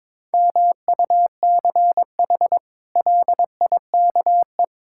Morse code, heard as a keyed tone, about 22 wpm, 700 Hz. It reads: MUCH LIKE
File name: practice / 22wpm / Sentences with 8-Word Phrases - 22wpm